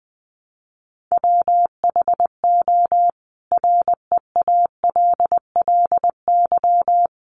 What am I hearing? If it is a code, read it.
Morse code, heard as a keyed tone, about 20 wpm, 700 Hz. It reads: WHO REALLY
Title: A Sound Of Thunder 20wpm